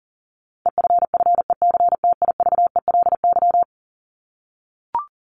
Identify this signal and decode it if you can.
Morse code, heard as a keyed tone, about 40 wpm, 700 Hz. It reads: EFFECTIVELY
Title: LY Suffix - Top 100 Words with Base Words 40wpm